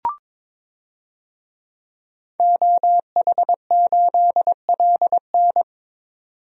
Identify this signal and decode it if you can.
Morse code, heard as a keyed tone, about 22 wpm, 700 Hz. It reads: OH8LN
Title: DXCC-100 Call Signs - 5 Characters - 22wpm